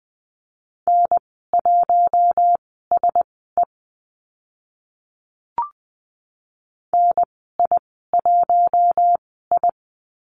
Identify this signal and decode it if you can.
Morse code, heard as a keyed tone, about 20 wpm, spaced as for 15 wpm, 700 Hz. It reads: N1SE NI1I
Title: 042 Lesson 8g - Number 1 - Call Signs 20wpm